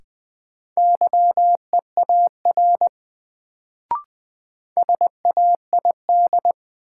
Morse code, heard as a keyed tone, about 20 wpm, 700 Hz. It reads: YEAR SAID
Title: Sets of 2 Words - New Word Drill - 20wpm